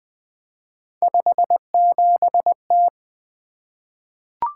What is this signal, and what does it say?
Morse code, heard as a keyed tone, about 20 wpm, 700 Hz. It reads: 57T
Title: Three Letter-Number Combinations - 20wpm